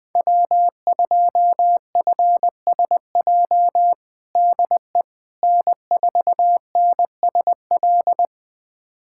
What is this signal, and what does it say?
Morse code, heard as a keyed tone, about 20 wpm, 700 Hz. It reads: W2FSJ DE N4NSL
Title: USA Callsign DE Callsign - 5 Characters - 20wpm